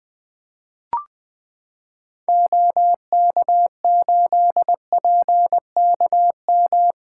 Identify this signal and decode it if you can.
Morse code, heard as a keyed tone, about 20 wpm, 700 Hz. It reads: OK8PKM
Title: 2023 SOTA Activator Call Signs - 20wpm